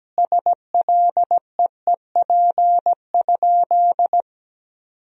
Morse code, heard as a keyed tone, about 17 wpm, 700 Hz. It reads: SLEEP?
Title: 3x Word-Spacing - Sentences from Top 500 Words - 17wpm